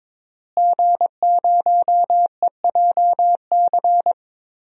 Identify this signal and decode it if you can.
Morse code, heard as a keyed tone, about 22 wpm, 700 Hz. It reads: G0EJC
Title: DXCC-100 Call Signs - 5 Characters - Repeated 3x - 22wpm